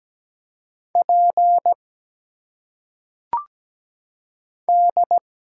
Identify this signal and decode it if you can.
Morse code, heard as a keyed tone, about 17 wpm, 700 Hz. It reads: P D
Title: Single Letters - Phonetic Pronunciation 17wpm